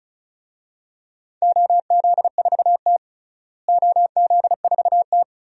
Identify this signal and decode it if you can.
Morse code, heard as a keyed tone, about 34 wpm, 700 Hz. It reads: OZ4T OZ4T
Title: DXCC-100 Call Signs - 4 Characters - Repeated 2x - 35wpm